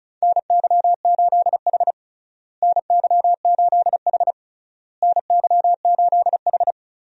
Morse code, heard as a keyed tone, about 36 wpm, 700 Hz. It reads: NY8H NY8H NY8H
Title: USA Call Signs - 4 Characters - Repeated 3x - 35wpm